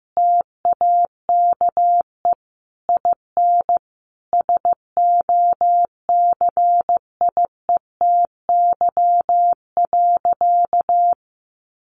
Morse code, heard as a keyed tone, about 15 wpm, 700 Hz. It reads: TAKE IN SOCIETY.
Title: Sentences from Top 1500 Words 15wpm